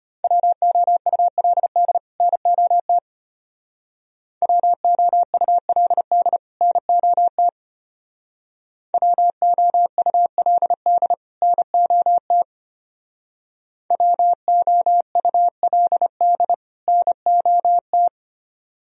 Morse code, heard as a keyed tone, about 32 wpm, 700 Hz. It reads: WOULD NOT WOULD NOT WOULD NOT WOULD NOT